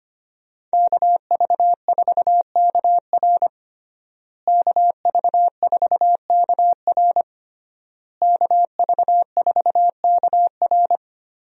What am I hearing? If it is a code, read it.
Morse code, heard as a keyed tone, about 25 wpm, 700 Hz. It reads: KV4KR KV4KR KV4KR
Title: USA Call Signs - 5 Characters - Repeated 3x - 25wpm